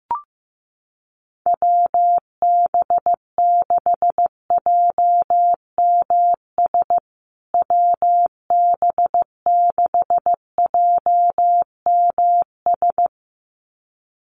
Morse code, heard as a keyed tone, about 15 wpm, 700 Hz. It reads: WB6JMS WB6JMS
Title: USA Call Signs Repeated 2x 15wpm